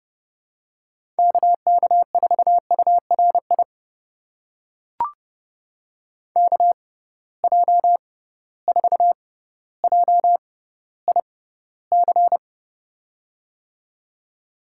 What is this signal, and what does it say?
Morse code, heard as a keyed tone, about 30 wpm, spaced as for 15 wpm, 700 Hz. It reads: KK4URI KJ4JIC
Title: USA Call Signs - 6x Character Spacing - 30wpm